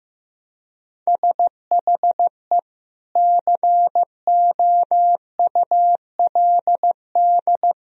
Morse code, heard as a keyed tone, about 15 wpm, 700 Hz. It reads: SHE COULD
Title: Sentences from Top 1000 Words - Review Part 1 - 15wpm